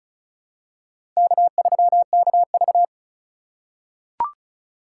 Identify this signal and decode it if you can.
Morse code, heard as a keyed tone, about 34 wpm, 700 Hz. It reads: K3KV
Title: USA Call Signs - 4 Characters - 3x Character Spacing - 35wpm